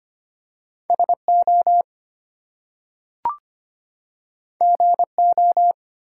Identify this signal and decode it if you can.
Morse code, heard as a keyed tone, about 25 wpm, 700 Hz. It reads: SO GO
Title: Two Letter Words - 25wpm